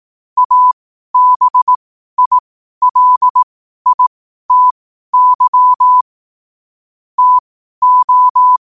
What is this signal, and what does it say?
Morse code, heard as a keyed tone, about 18 wpm, spaced as for 12 wpm, 1000 Hz. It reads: ABILITY TO
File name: qotdfiles / 12wpm